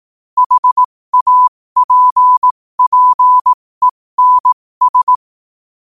Morse code, heard as a keyed tone, about 18 wpm, spaced as for 15 wpm, 1000 Hz. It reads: HAPPENS
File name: qotdfiles / 15wpm